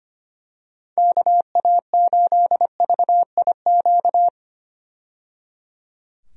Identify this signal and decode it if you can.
Morse code, heard as a keyed tone, about 25 wpm, 700 Hz. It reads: KA8VIQ
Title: USA Call Signs - 6 Characters - 25wpm